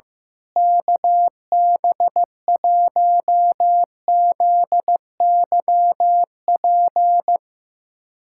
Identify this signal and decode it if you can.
Morse code, heard as a keyed tone, about 15 wpm, 700 Hz. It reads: KB1ZYP
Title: USA Callsign DE Callsign 15wpm